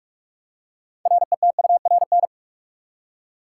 Morse code, heard as a keyed tone, about 46 wpm, 700 Hz. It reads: RETURN
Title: Six Letter Words - 45wpm